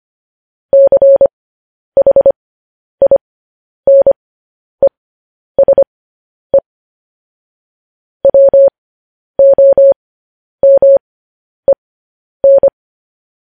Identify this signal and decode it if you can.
Morse code, heard as a keyed tone, about 25 wpm, spaced as for 10 wpm, 550 Hz. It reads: CHINESE WOMEN